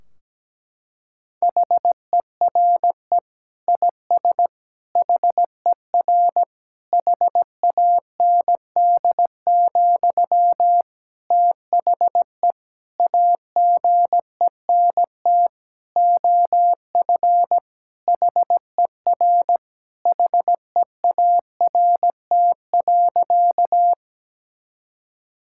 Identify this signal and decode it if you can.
Morse code, heard as a keyed tone, about 17 wpm, 700 Hz. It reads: HERE IS HER HAND, THE AGENT OF HER HEART.
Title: Sentences from Top 1600 Words 17wpm